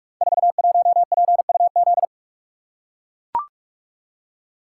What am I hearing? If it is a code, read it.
Morse code, heard as a keyed tone, about 44 wpm, 700 Hz. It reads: 41PUZ
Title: Code Groups - 5 Letter:Numbers - 45wpm